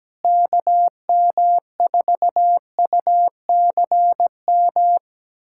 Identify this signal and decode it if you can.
Morse code, heard as a keyed tone, about 17 wpm, 700 Hz. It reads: KM4UCM
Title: USA Callsign DE Callsign - 6 Characters - 17wpm